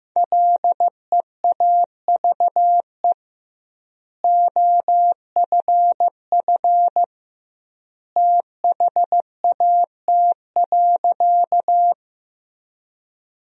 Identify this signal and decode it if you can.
Morse code, heard as a keyed tone, about 15 wpm, 700 Hz. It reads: LEAVE OFF THAT.
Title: 2x Word-Spacing - Sentences from Top 400 Words - 15wpm